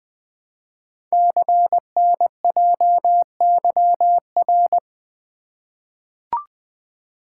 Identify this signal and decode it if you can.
Morse code, heard as a keyed tone, about 20 wpm, 700 Hz. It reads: CNJYR
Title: Code Groups - 5 Letters - 20wpm